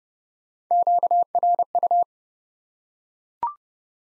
Morse code, heard as a keyed tone, about 32 wpm, 700 Hz. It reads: QRU
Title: Top 90 Abbr 30WPM